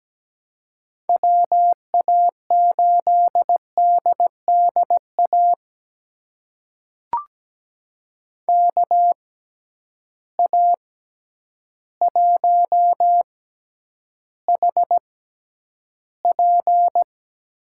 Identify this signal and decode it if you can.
Morse code, heard as a keyed tone, about 17 wpm, spaced as for 9 wpm, 700 Hz. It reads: WA8DDA KA1HP
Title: USA Call Signs - 6 Characters - 6x Character Spacing - 17wpm